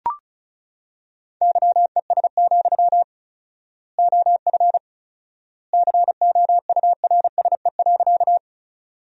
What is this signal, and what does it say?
Morse code, heard as a keyed tone, about 36 wpm, 700 Hz. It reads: YES, OF COURSE.
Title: 4x Word-Spacing - Sentences from Top 500 Words - 35wpm